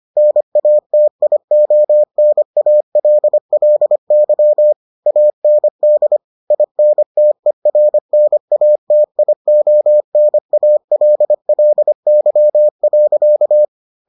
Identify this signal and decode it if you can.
Morse code, heard as a keyed tone, about 25 wpm, 600 Hz. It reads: NATIONALLY AND INTERNATIONALLY.